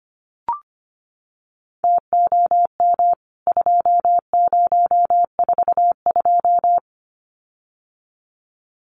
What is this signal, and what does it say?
Morse code, heard as a keyed tone, about 25 wpm, 700 Hz. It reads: TOM 2042
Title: CWT Exchange 25wpm